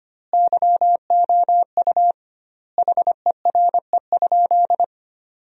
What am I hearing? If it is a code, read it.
Morse code, heard as a keyed tone, about 25 wpm, 700 Hz. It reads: YOU HERE?
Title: 2x Word-Spacing - Sentences from Top 500 Words - 25wpm